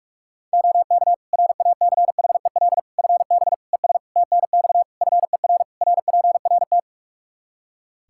Morse code, heard as a keyed tone, about 46 wpm, 700 Hz. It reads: OK RACHEL FB ES TNX FER RPRT
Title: Practice QSO - Exchange 2 - Info - 45wpm